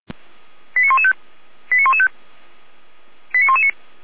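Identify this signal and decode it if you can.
Five-tone selective call (ZVEI), heard as quick runs of five tones, tones of about 70 ms. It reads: 89115 89115 89119
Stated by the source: Five-tone ZVEI selective call received by SDR, Feuerwehr Orlamünde.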